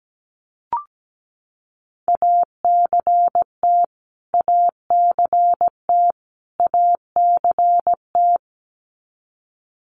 Morse code, heard as a keyed tone, about 17 wpm, 700 Hz. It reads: ACT ACT ACT
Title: Top 500 Words Repeated 17wpm